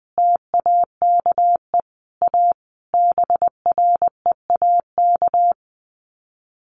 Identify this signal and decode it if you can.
Morse code, heard as a keyed tone, about 20 wpm, 700 Hz. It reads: TAKE A BREAK
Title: Sets of 5 Words 20wpm